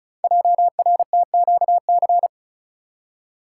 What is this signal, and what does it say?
Morse code, heard as a keyed tone, about 36 wpm, 700 Hz. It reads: JRTQC
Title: Code Groups - 5 Letters - 35wpm